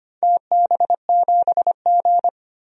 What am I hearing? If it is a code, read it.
Morse code, heard as a keyed tone, about 25 wpm, 700 Hz. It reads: TB7G
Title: DXCC-100 Call Signs - 4 Characters - 25wpm